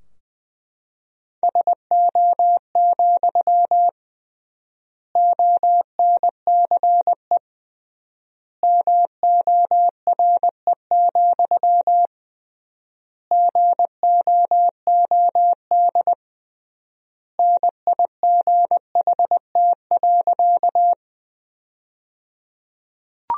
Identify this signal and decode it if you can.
Morse code, heard as a keyed tone, about 20 wpm, 700 Hz. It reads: SO, ONCE MORE, GOOD NIGHT.
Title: 3x Word-Spacing - Sentences from Top 300 Words - 20wpm